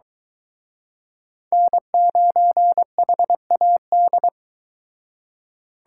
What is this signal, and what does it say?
Morse code, heard as a keyed tone, about 23 wpm, 700 Hz. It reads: N9HAD